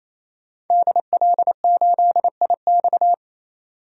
Morse code, heard as a keyed tone, about 27 wpm, 700 Hz. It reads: DL8IX
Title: DXCC-100 Call Signs - 5 Characters - 28wpm